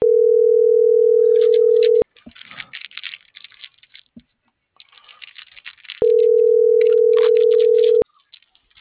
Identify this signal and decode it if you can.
Ringback tone (North America), 440 + 480 Hz.